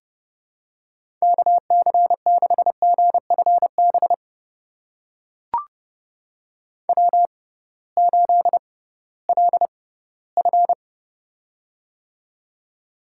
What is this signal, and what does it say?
Morse code, heard as a keyed tone, about 30 wpm, spaced as for 18 wpm, 700 Hz. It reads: KC6GFB W8LF
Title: USA Call Signs - 6x Character Spacing - 30wpm